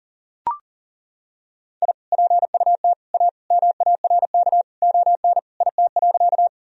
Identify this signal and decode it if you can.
Morse code, heard as a keyed tone, about 37 wpm, 700 Hz. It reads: I PUT A MARK ON IT.
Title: Sentences from Top 1000 Words - Review Part 1 - 40wpm